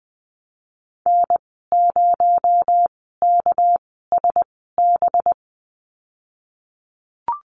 Morse code, heard as a keyed tone, about 20 wpm, spaced as for 15 wpm, 700 Hz. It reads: N0KSB
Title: 238 Lesson 38g - Letter q - Call Signs 20wpm